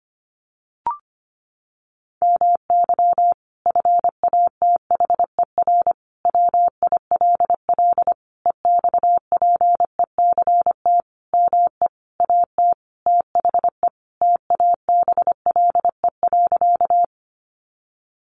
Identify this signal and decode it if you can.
Morse code, heard as a keyed tone, about 25 wpm, 700 Hz. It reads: MY FATHER WILL EXPECT ME AT THE TABLE.
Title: Sentences from Top 1100 Words 25wpm